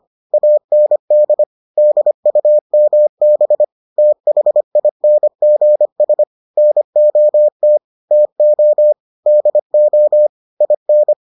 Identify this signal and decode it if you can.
Morse code, heard as a keyed tone, about 25 wpm, 600 Hz. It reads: AND DUMB THINGS NOT TO DO IN